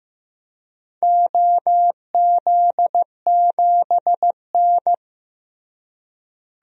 Morse code, heard as a keyed tone, about 15 wpm, 700 Hz. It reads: OZ7N
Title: DXCC-100 Call Signs - 4 Characters - Repeated 3x - 15wpm